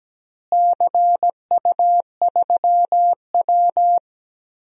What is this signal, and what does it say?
Morse code, heard as a keyed tone, about 17 wpm, 700 Hz. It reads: CU3W
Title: DXCC-100 Call Signs - 4 Characters - Repeated 3x - 17wpm